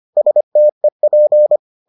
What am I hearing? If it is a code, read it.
Morse code, heard as a keyed tone, about 25 wpm, 600 Hz. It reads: STEP